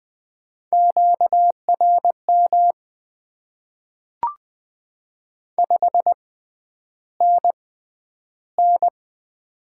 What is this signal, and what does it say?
Morse code, heard as a keyed tone, about 20 wpm, spaced as for 10 wpm, 700 Hz. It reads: QRM 5NN
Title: Top 90 QSO Elements - 6x Character Spacing - 20wpm